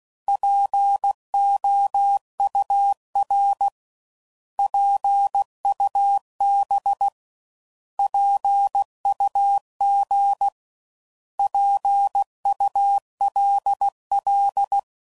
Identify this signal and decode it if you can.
Morse code, heard as a keyed tone, about 16 wpm, 800 Hz. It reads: POUR PUB PUG PULL
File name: Prefixes / By First Letter / Ps